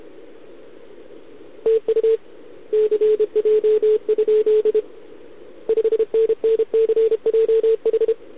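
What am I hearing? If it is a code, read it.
Morse code, heard as a keyed tone, about 30 wpm, 440 Hz. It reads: TU CJ? 5NN CJH
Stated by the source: Contest CW from OK1RI on 80 m during the OKOM DX contest, recorded by a QRP station using a KX3 and a wire antenna.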